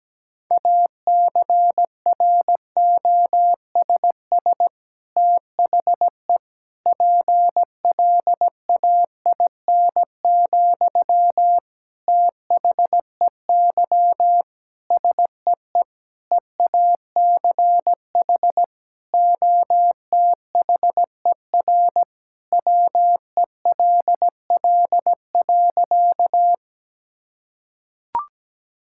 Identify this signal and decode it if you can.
Morse code, heard as a keyed tone, about 17 wpm, 700 Hz. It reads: ACROSS THE PLAIN, THEY SEE EACH OTHER WELL.
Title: Sentences from Top 1000 Words - Review Part 5 - 17wpm